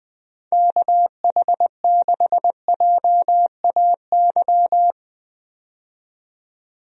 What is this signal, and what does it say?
Morse code, heard as a keyed tone, about 20 wpm, 700 Hz. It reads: KH6JAY
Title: DXCC-100 Call Signs - 6 Characters - 20wpm